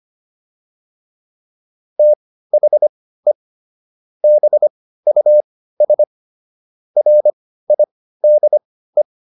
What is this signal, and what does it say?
Morse code, heard as a keyed tone, about 25 wpm, spaced as for 15 wpm, 600 Hz. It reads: THE BUS RIDE